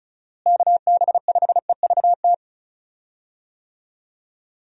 Morse code, heard as a keyed tone, about 36 wpm, 700 Hz. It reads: KB5EVT
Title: USA Call Signs - 6 Characters - Repeated 3x - 35wpm